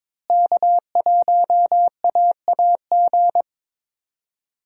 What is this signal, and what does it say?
Morse code, heard as a keyed tone, about 22 wpm, 700 Hz. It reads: K1AAG
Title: USA Call Signs - 5 Characters - Repeated 3x - 22wpm